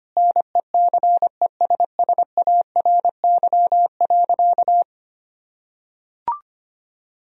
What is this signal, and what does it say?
Morse code, heard as a keyed tone, about 25 wpm, 700 Hz. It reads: NECESSARY.
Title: Sentences from Top 1000 Words - Review Part 4 - 25wpm